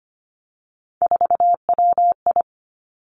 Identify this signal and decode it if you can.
Morse code, heard as a keyed tone, about 25 wpm, 700 Hz. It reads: 4WI
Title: Three Challenging Letter-Number Combinations 25wpm